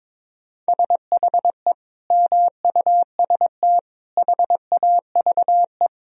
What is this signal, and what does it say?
Morse code, heard as a keyed tone, about 22 wpm, 700 Hz. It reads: SHE MUST HAVE
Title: Sentences from Top 1000 Words - Review Part 2 - 22wpm